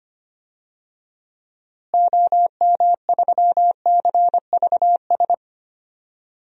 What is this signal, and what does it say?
Morse code, heard as a keyed tone, about 25 wpm, 700 Hz. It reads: OM3CVS
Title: DXCC-100 Call Signs - 6 Characters - Repeated 3x - 25wpm